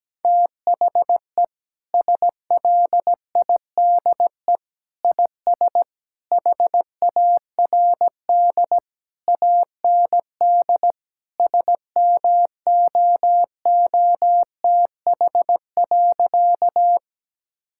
Morse code, heard as a keyed tone, about 17 wpm, 700 Hz. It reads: THE SLIDE IS HARD AND SMOOTH.
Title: Sentences from Top 1600 Words 17wpm